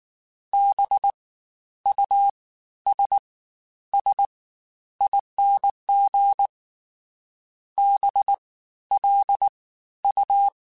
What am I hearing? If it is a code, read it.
Morse code, heard as a keyed tone, about 19 wpm, 800 Hz. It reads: B U S S ING B L U